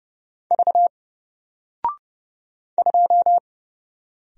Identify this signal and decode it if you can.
Morse code, heard as a keyed tone, about 30 wpm, 700 Hz. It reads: V 2
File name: Single-Letter-Number-30wpm